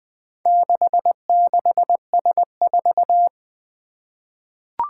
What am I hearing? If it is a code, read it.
Morse code, heard as a keyed tone, about 20 wpm, 700 Hz. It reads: 66S4
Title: Four Challenging Letter-Number Combinations 20wpm